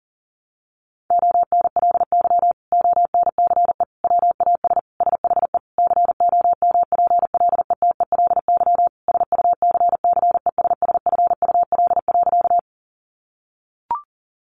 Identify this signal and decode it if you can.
Morse code, heard as a keyed tone, about 40 wpm, 700 Hz. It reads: ONLY ONCE WAS SHE COMPLETELY SUCCESSFUL.
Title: Sentences from Top 1100 Words 40wpm